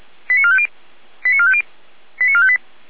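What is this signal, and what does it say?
Five-tone selective call (ZVEI), heard as quick runs of five tones, tones of about 70 ms. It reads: 89450 89450 89458
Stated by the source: ZVEI five-tone selective call received on SDR, from the audio archive of feuerwehr-orlamuende (fire brigade).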